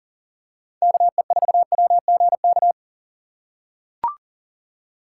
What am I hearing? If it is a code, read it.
Morse code, heard as a keyed tone, about 40 wpm, 700 Hz. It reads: KE4WGK
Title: USA Call Signs - 6 Characters - 6x Character Spacing - 40wpm